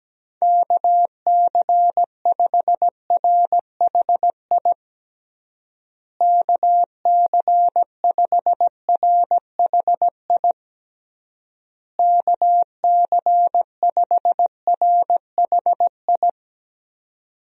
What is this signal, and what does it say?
Morse code, heard as a keyed tone, about 17 wpm, 700 Hz. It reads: KC5RHI KC5RHI KC5RHI
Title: USA Call Signs - 6 Characters - Repeated 3x - 17wpm